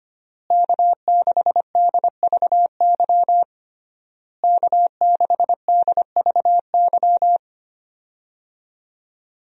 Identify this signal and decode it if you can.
Morse code, heard as a keyed tone, about 25 wpm, 700 Hz. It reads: K6DVY K6DVY
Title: USA Call Signs - 5 Characters - Repeated 3x - 25wpm